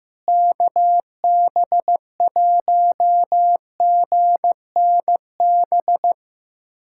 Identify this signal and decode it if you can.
Morse code, heard as a keyed tone, about 15 wpm, 700 Hz. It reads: KB1GNB
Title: USA Callsign DE Callsign 15wpm